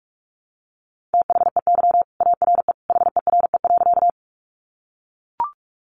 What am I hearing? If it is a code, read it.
Morse code, heard as a keyed tone, about 44 wpm, 700 Hz. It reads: THEY ARE HERE.
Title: Sentences from Top 200 Words 45wpm